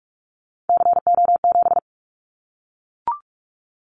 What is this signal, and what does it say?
Morse code, heard as a keyed tone, about 44 wpm, 700 Hz. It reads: CO7
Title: Callsign Prefixes for Top 100 DXCC - 45wpm